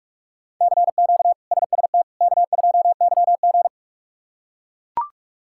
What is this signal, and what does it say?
Morse code, heard as a keyed tone, about 41 wpm, 700 Hz. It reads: CQ SST K2YG
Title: K1USN SST QSO Practice - 45wpm